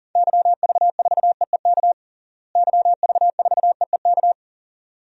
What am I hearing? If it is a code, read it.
Morse code, heard as a keyed tone, about 40 wpm, 700 Hz. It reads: YV4EEK YV4EEK
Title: DXCC-100 Call Signs - 6 Characters - Repeated 2x - 40wpm